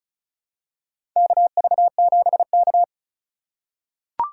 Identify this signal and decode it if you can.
Morse code, heard as a keyed tone, about 36 wpm, 700 Hz. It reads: KV7K
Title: USA Call Signs - 2x Character Spacing - 35wpm